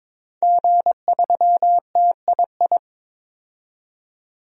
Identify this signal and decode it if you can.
Morse code, heard as a keyed tone, about 22 wpm, 700 Hz. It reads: G3TII
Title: DXCC-100 Call Signs - 5 Characters - 22wpm